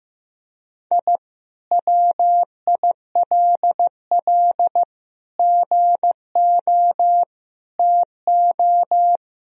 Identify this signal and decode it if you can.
Morse code, heard as a keyed tone, about 15 wpm, 700 Hz. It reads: I WILL GO TO
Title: Sentences from Top 1000 Words - Review Part 1 - 15wpm